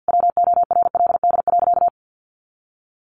Morse code, heard as a keyed tone, about 50 wpm, 700 Hz. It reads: WORLD.
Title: Sentences from Top 200 Words 50wpm